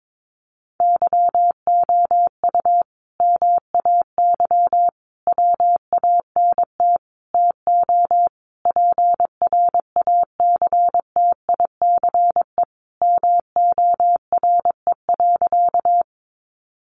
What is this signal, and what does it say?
Morse code, heard as a keyed tone, about 22 wpm, 700 Hz. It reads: YOU MAY WANT TO PRACTICE MORE.
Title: Sentences from Top 700 Words 22wpm